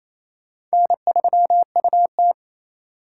Morse code, heard as a keyed tone, about 34 wpm, 700 Hz. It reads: N3UT
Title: Speed Racing - USA 1x2 Call Signs 28wpm